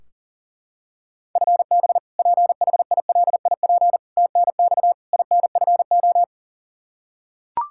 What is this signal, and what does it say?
Morse code, heard as a keyed tone, about 37 wpm, 700 Hz. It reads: FB PHILIP TNX INFO
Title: Practice QSO - Exchange 3 - More Info - 40wpm